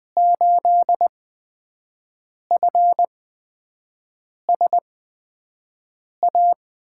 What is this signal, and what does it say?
Morse code, heard as a keyed tone, about 20 wpm, spaced as for 5 wpm, 700 Hz. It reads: 8FSA
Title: USA Call Signs - 6 Characters - 8x Character Spacing - 20wpm